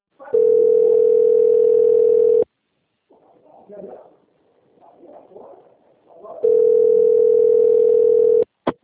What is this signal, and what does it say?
Ringback tone (North America), 440 + 480 Hz, with speech.
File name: out- - - - - .3294